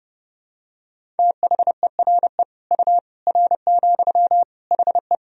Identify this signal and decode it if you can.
Morse code, heard as a keyed tone, about 32 wpm, 700 Hz. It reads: THERE U R, HE
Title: Sentences from Top 100 Words and Abbreviations 30wpm